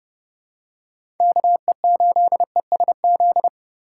Morse code, heard as a keyed tone, about 30 wpm, 700 Hz. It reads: KE8ESZ
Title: USA Call Signs - 6 Characters - 4x Character Spacing - 30wpm